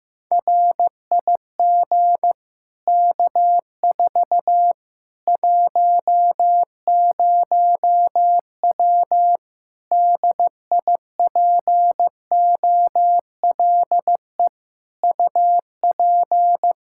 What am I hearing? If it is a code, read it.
Morse code, heard as a keyed tone, about 15 wpm, 700 Hz. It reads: RIG K4 10W DIPOLE UP
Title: Practice QSO - Exchange 2 - Info - 15wpm